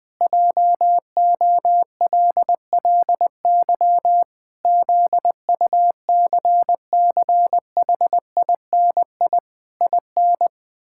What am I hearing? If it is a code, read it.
Morse code, heard as a keyed tone, about 20 wpm, 700 Hz. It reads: JOLLY ZUCCHINI IN
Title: Pangram Challenge v2 - 20wpm